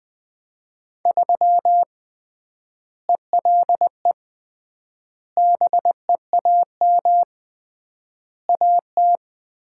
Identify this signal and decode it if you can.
Morse code, heard as a keyed tone, about 20 wpm, 700 Hz. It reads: 3 ELE BEAM AT
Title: Common QSO Phrases - 3x Word-Spacing - 20wpm